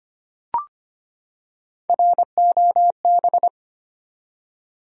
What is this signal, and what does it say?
Morse code, heard as a keyed tone, about 25 wpm, 700 Hz. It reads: ROB
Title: USA Common Names - 25wpm